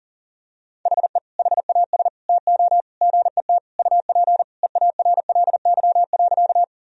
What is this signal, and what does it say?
Morse code, heard as a keyed tone, about 40 wpm, 700 Hz. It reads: HE HAS TO GET UP EARLY.
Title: Sentences from Top 1000 Words - Review Part 2 - 40wpm